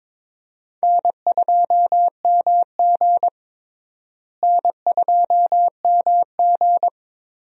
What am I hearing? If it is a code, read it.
Morse code, heard as a keyed tone, about 22 wpm, 700 Hz. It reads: N2MG N2MG
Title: USA Call Signs - 4 Characters - Repeated 3x - 22wpm